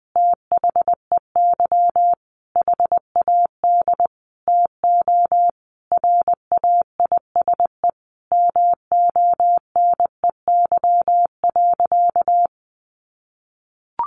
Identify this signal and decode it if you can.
Morse code, heard as a keyed tone, about 20 wpm, 700 Hz. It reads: THEY HAD TO RAISE MONEY.
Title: Sentences from Top 1400 Words 20wpm